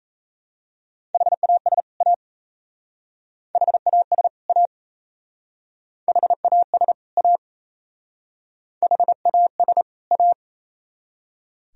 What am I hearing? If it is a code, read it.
Morse code, heard as a keyed tone, about 37 wpm, 700 Hz. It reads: HAS A HAS A HAS A HAS A